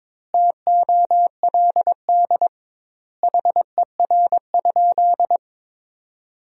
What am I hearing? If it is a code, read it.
Morse code, heard as a keyed tone, about 22 wpm, 700 Hz. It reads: TOLD HER?
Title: 2x Word-Spacing - Sentences from Top 400 Words - 22wpm